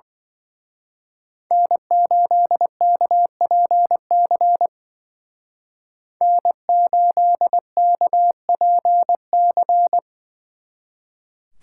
Morse code, heard as a keyed tone, about 24 wpm, 700 Hz. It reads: N8KPC N8KPC
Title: Speed Racing - USA Call Signs 20wpm